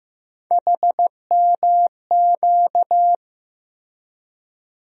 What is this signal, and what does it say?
Morse code, heard as a keyed tone, about 15 wpm, 700 Hz. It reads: HMQ
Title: USA Call Sign Postfixes - 3 Letters 15wpm